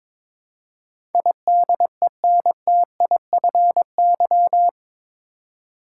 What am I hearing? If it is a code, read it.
Morse code, heard as a keyed tone, about 22 wpm, 700 Hz. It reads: IDENTIFY
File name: practice / 22wpm / Eight Letter Words - 22wpm